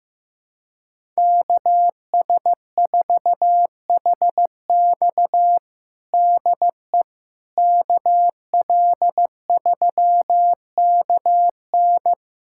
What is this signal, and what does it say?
Morse code, heard as a keyed tone, about 15 wpm, 700 Hz. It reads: KS4HX DE KL3KN
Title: USA Callsign DE Callsign - 5 Characters - 15wpm